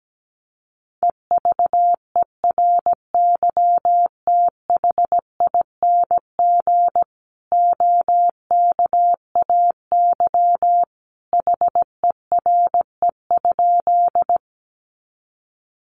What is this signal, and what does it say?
Morse code, heard as a keyed tone, about 17 wpm, 700 Hz. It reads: EVERYTHING OKAY HERE?
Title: A Sound Of Thunder 17wpm